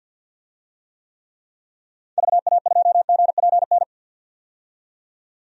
Morse code, heard as a keyed tone, about 50 wpm, 700 Hz. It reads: VA2GPN
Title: DXCC-100 Call Signs - 6 Characters - 50wpm